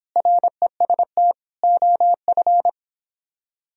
Morse code, heard as a keyed tone, about 26 wpm, 700 Hz. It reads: REST OF